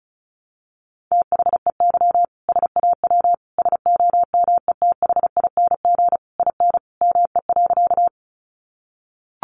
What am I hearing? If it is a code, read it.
Morse code, heard as a keyed tone, about 34 wpm, 700 Hz. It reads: THEY SAW SOMETHING IN ME.
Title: Sentences from Top 600 Words 35wpm